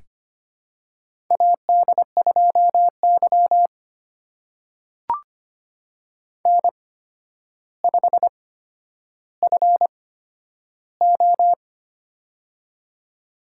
Morse code, heard as a keyed tone, about 25 wpm, spaced as for 11 wpm, 700 Hz. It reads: AD2Y N5FO
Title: USA Call Signs - 4 Characters - 8x Character Spacing - 25wpm